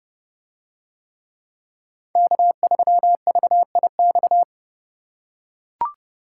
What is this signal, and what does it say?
Morse code, heard as a keyed tone, about 30 wpm, 700 Hz. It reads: K3VIX